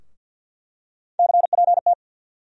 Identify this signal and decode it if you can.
Morse code, heard as a keyed tone, about 48 wpm, 700 Hz. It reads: CPT
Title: USA Call Sign Postfixes - 3 Letters 50wpm